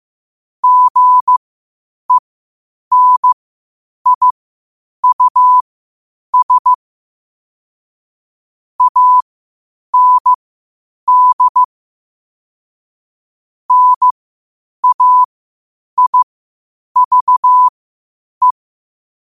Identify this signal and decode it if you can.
Morse code, heard as a keyed tone, about 15 wpm, spaced as for 8 wpm, 1000 Hz. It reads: GENIUS AND NAIVE
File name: qotdfiles / 8wpm